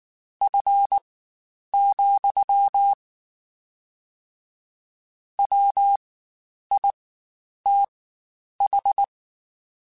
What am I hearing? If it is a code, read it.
Morse code, heard as a keyed tone, about 19 wpm, spaced as for 9 wpm, 800 Hz. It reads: F, WITH
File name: Spaced Characters / Spaced Characters-mp3s / Weather5 12 sp @18